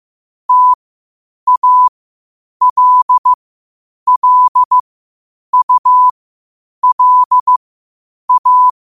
Morse code, heard as a keyed tone, about 15 wpm, spaced as for 8 wpm, 1000 Hz. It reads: TALLULA
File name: 8wpm